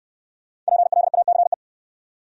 Morse code, heard as a keyed tone, about 67 wpm, 700 Hz. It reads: PLACE